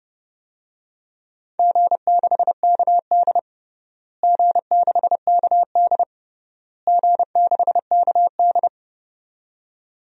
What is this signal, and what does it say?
Morse code, heard as a keyed tone, about 30 wpm, 700 Hz. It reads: G6KD G6KD G6KD
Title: DXCC-100 Call Signs - 4 Characters - Repeated 3x - 30wpm